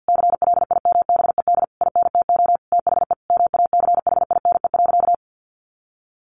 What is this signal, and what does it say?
Morse code, heard as a keyed tone, about 49 wpm, 700 Hz. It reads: CLIMBED INTO THE MACHINE.
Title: A Sound Of Thunder 50wpm